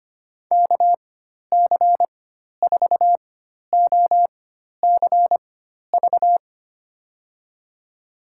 Morse code, heard as a keyed tone, about 25 wpm, spaced as for 12 wpm, 700 Hz. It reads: KC4OCV
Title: USA Call Signs - 6 Characters - 4x Character Spacing - 25wpm